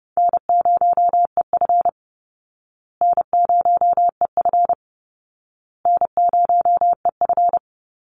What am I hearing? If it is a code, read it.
Morse code, heard as a keyed tone, about 30 wpm, 700 Hz. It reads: N0EF N0EF N0EF
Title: Call Signs Repeated 2x - 30wpm